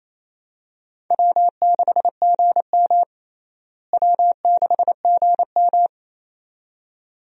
Morse code, heard as a keyed tone, about 28 wpm, 700 Hz. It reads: W6GM W6GM
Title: USA Call Signs - 4 Characters - Repeated 2x - 28wpm